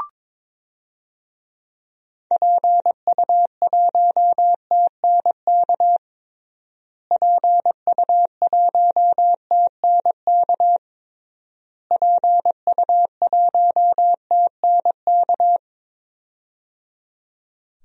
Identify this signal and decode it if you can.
Morse code, heard as a keyed tone, about 21 wpm, 700 Hz. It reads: PU1TNK PU1TNK PU1TNK
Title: DXCC-100 Call Signs - 6 Characters - Repeated 3x - 22wpm